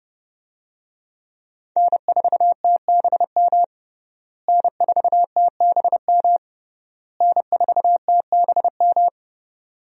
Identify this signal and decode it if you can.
Morse code, heard as a keyed tone, about 30 wpm, 700 Hz. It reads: N4TBM N4TBM N4TBM
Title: USA Call Signs - 5 Characters - Repeated 3x - 30wpm